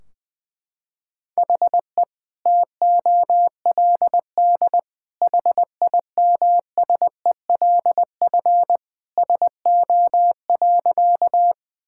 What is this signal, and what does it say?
Morse code, heard as a keyed tone, about 20 wpm, 700 Hz. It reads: HE TOLD HIMSELF SO.
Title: Sentences from Top 1600 Words 20wpm